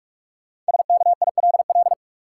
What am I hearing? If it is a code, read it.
Morse code, heard as a keyed tone, about 41 wpm, 700 Hz. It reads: SKILL
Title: Sentences from Top 1200 Words - New Word Drill - 45wpm